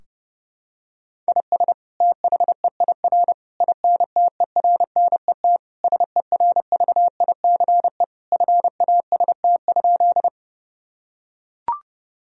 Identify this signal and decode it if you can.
Morse code, heard as a keyed tone, about 30 wpm, 700 Hz. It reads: IS THEIR INTERNET SERVICE FAST?
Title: Sentences from Top 1000 Words - Review Part 4 - 30wpm